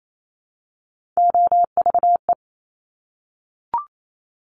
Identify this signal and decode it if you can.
Morse code, heard as a keyed tone, about 28 wpm, 700 Hz. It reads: OVE
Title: Top Three Letter Combinations in Words 28wpm